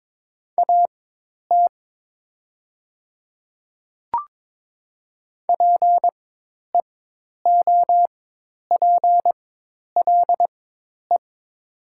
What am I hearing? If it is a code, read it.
Morse code, heard as a keyed tone, about 22 wpm, spaced as for 10 wpm, 700 Hz. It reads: AT PEOPLE
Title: 4x Character Spacing - Top 100 Words - 22wpm